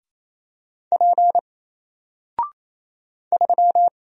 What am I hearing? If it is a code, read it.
Morse code, heard as a keyed tone, about 27 wpm, 700 Hz. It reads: P 3
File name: Single-Letter-Number-28wpm